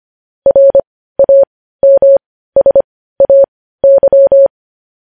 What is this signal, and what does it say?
Morse code, heard as a keyed tone, about 25 wpm, spaced as for 15 wpm, 550 Hz. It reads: RAMSAY
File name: podcast / News headlines-1x-25x15